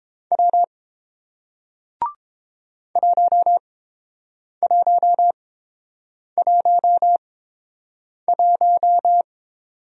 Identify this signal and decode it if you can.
Morse code, heard as a keyed tone, about 29 wpm, spaced as for 7 wpm, 700 Hz. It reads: W 1111